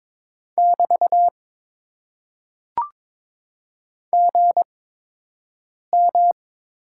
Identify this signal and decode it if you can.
Morse code, heard as a keyed tone, about 22 wpm, spaced as for 6 wpm, 700 Hz. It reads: <BT> GM
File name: Top 90 QSO Elements - 8x Character Spacing - 22wpm